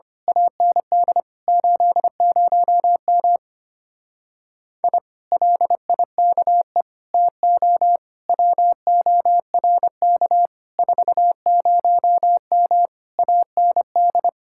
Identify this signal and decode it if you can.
Morse code, heard as a keyed tone, about 25 wpm, 700 Hz. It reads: AND 80M I LIKE TO WORK 40M AND
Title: Speed-Racing Common QSO Phrases Encore - 25wpm